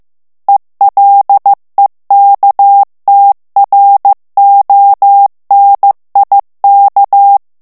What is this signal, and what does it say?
Morse code, heard as a keyed tone, about 15 wpm, 800 Hz. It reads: ELEKTRONIK
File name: ALLEMAND texte clair 1 vitesse 14 mots minute